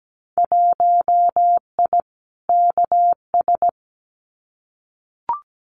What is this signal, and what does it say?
Morse code, heard as a keyed tone, about 17 wpm, 700 Hz. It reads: 1I KS
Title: Winter Field Day Exchanges Encore 17wpm